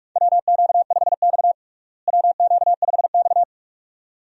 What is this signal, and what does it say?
Morse code, heard as a keyed tone, about 46 wpm, 700 Hz. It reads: WQ5X WQ5X
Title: USA Call Signs - 4 Characters - Repeated 2x - 45wpm